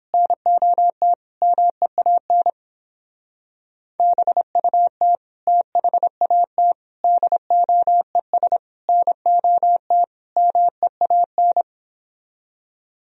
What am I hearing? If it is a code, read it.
Morse code, heard as a keyed tone, about 29 wpm, 700 Hz. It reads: NOT MEAN BUT THAT DOES NOT MEAN